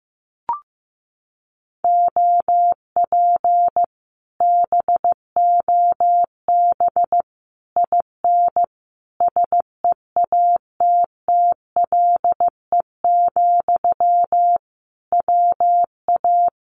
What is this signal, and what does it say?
Morse code, heard as a keyed tone, about 15 wpm, 700 Hz. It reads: OP BOB IN SEATTLE, WA
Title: Common QSO Phrases Encore 15wpm